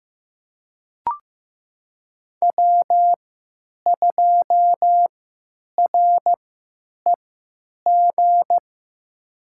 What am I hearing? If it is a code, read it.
Morse code, heard as a keyed tone, about 15 wpm, spaced as for 9 wpm, 700 Hz. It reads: W2REG
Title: USA Call Signs - 5 Characters - 3x Character Spacing - 15wpm